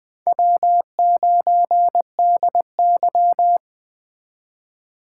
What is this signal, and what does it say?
Morse code, heard as a keyed tone, about 21 wpm, 700 Hz. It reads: W9DY
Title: K1USN SST QSO Practice - 20wpm